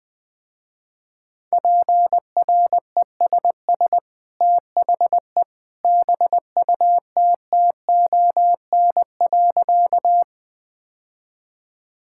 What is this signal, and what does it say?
Morse code, heard as a keyed tone, about 21 wpm, 700 Hz. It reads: PRESS THE BUTTON.
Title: Easy Sentences with Hints - 20wpm